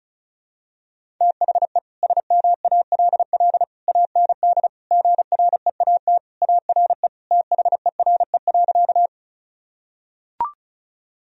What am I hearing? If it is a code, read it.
Morse code, heard as a keyed tone, about 36 wpm, 700 Hz. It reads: THE SMALL AND GREAT ARE THERE.
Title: Sentences from Top 1000 Words - Review Part 1 - 35wpm